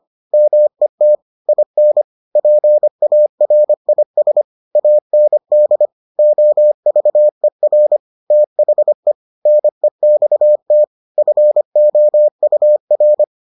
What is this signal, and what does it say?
Morse code, heard as a keyed tone, about 25 wpm, 600 Hz. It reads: MET IN PARIS AND OVER THE NEXT FOUR